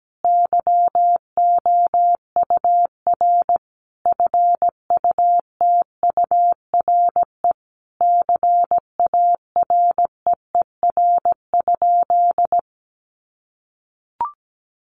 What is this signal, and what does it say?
Morse code, heard as a keyed tone, about 17 wpm, 700 Hz. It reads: YOUR FUTURE CAREER?
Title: Sentences from Top 1000 Words 17wpm